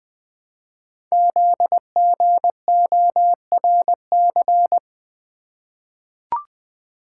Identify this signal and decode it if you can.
Morse code, heard as a keyed tone, about 20 wpm, 700 Hz. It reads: ZGORC
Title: Code Groups - 5 Letters - 20wpm